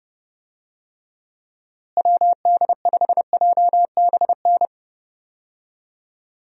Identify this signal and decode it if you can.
Morse code, heard as a keyed tone, about 30 wpm, 700 Hz. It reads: WD5JBN
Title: USA Call Signs - 6 Characters - Repeated 2x - 30wpm